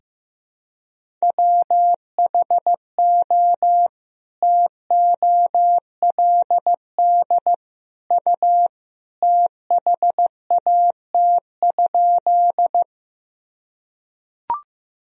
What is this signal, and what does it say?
Morse code, heard as a keyed tone, about 15 wpm, 700 Hz. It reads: WHO TOLD U THAT?
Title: Sentences from Top 400 Words and Abbreviations 15wpm